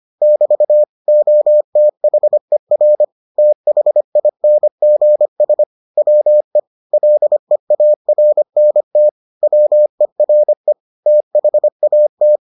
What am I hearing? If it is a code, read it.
Morse code, heard as a keyed tone, about 25 wpm, 600 Hz. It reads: <BT> OTHER THINGS WE LEARNT WERE THAT